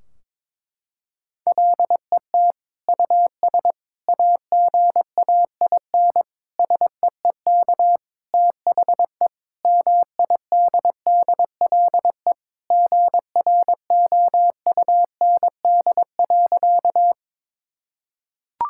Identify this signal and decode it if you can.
Morse code, heard as a keyed tone, about 22 wpm, 700 Hz. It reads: LET US AGAIN SEEK THE MIDDLE GROUND.
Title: Sentences from Top 1600 Words 22wpm